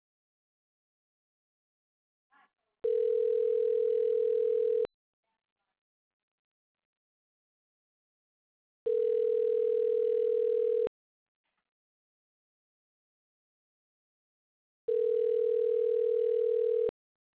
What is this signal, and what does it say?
Ringback tone (North America), 440 + 480 Hz.